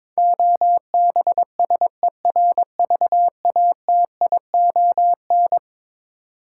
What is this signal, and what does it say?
Morse code, heard as a keyed tone, about 22 wpm, 700 Hz. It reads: OBSERVATION
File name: practice / 22wpm / Eleven Letter Words - 22wpm